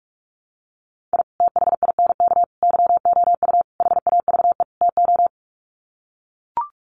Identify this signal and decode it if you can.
Morse code, heard as a keyed tone, about 44 wpm, 700 Hz. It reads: I THINK YOU HAVE TO
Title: Sets of 5 Words 45wpm